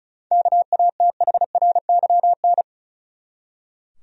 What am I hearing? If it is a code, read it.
Morse code, heard as a keyed tone, about 36 wpm, 700 Hz. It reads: KATHRYN
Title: USA Common Names - 35wpm